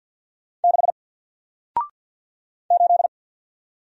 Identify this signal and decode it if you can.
Morse code, heard as a keyed tone, about 48 wpm, 700 Hz. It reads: 6 8
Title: Single Letter-Number Alt 50wpm